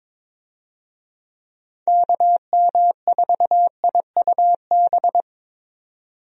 Morse code, heard as a keyed tone, about 22 wpm, 700 Hz. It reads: KM4IUB
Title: USA Call Signs - 6 Characters - 22wpm